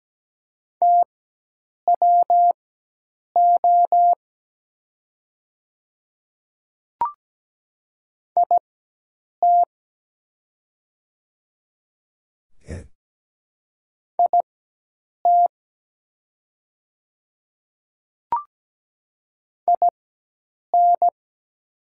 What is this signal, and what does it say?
Morse code, heard as a keyed tone, about 17 wpm, spaced as for 8 wpm, 700 Hz. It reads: TWO IT IT IN
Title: 4x Character Spacing - Top 100 Words - 17wpm